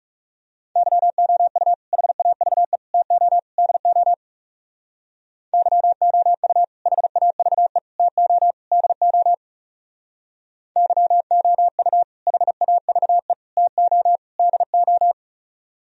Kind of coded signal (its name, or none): Morse code